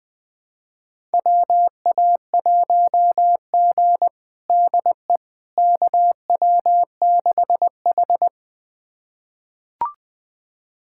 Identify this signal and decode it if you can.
Morse code, heard as a keyed tone, about 20 wpm, 700 Hz. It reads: WA1G DE KW6H
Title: USA Callsign DE Callsign - 4 Characters - 20wpm